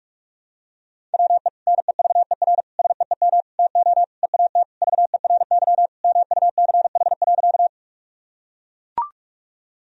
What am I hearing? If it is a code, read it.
Morse code, heard as a keyed tone, about 41 wpm, 700 Hz. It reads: WE NEVER SEEM TO EAT VERY MUCH.
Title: Sentences from Top 1000 Words - Review Part 1 - 45wpm